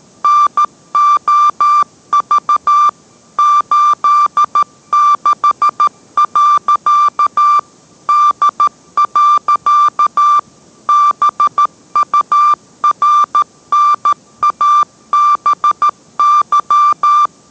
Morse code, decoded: NOV 86. D. BURNABY

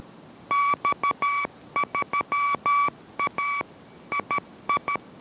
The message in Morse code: X3A II

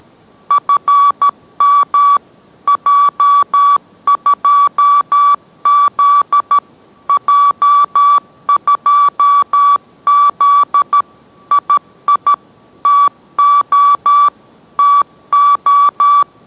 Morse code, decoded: FM J2Z J2Z II TO TO

